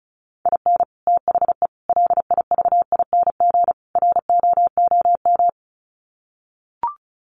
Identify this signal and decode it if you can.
Morse code, heard as a keyed tone, about 34 wpm, 700 Hz. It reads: IN THE LIVING ROOM
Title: Sets of 4 Words 35wpm